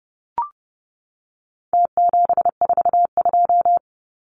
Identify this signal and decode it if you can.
Morse code, heard as a keyed tone, about 30 wpm, 700 Hz. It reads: T742
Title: Numbers - Contest Style - 4-Digit 30wpm